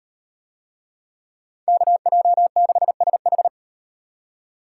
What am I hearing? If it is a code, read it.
Morse code, heard as a keyed tone, about 36 wpm, 700 Hz. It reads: KJ6SH